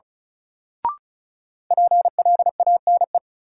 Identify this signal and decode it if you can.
Morse code, heard as a keyed tone, about 36 wpm, 700 Hz. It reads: PLANE
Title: Sentences from Top 1000 Words - Review Part 3 - New Word Drill - 35wpm